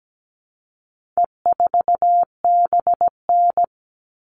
Morse code, heard as a keyed tone, about 17 wpm, 700 Hz. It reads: E4BN
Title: All Cut Numbers - 4-Digit 17wpm